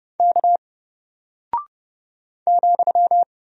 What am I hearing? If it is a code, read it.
Morse code, heard as a keyed tone, about 29 wpm, 700 Hz. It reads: K ,
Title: Single Character (Letter:Number:Punctuation:Prosign) Alt 30wpm